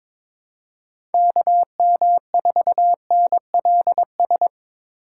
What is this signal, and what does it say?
Morse code, heard as a keyed tone, about 22 wpm, 700 Hz. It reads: KM4NLS
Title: USA Call Signs - 6 Characters - 2x Character Spacing - 22wpm